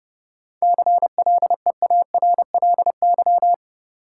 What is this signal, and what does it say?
Morse code, heard as a keyed tone, about 30 wpm, 700 Hz. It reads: CLEARLY
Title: Seven Letter Words - 30wpm